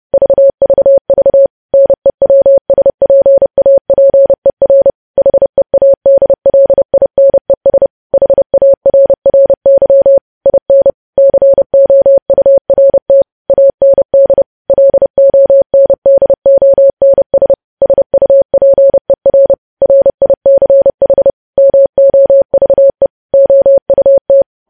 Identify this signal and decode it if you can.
Morse code, heard as a keyed tone, about 30 wpm, 550 Hz. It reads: VVV NEWSPAPER HEADLINES HARRY IN COURT AND LONDONS SUPER RICH MOVE OUT
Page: news headlines in morse code at 30 wpm